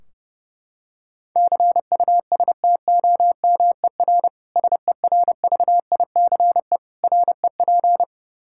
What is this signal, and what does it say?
Morse code, heard as a keyed tone, about 31 wpm, 700 Hz. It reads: CUSTOMER SERVICE REP
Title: Practice QSO - Exchange 3 - More Info - 30wpm